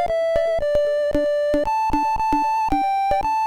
There is no code.